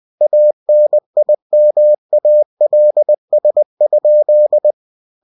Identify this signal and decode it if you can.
Morse code, heard as a keyed tone, about 20 wpm, 600 Hz. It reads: ANIMALS?
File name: cw-books / winnie-the-pooh / winnie-20wpm-page-0022